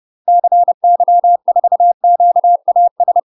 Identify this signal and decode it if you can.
Morse code, heard as a keyed tone, about 30 wpm, 700 Hz. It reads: CY4QAS
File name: UK MASTER QSOs TRAINING/30WPM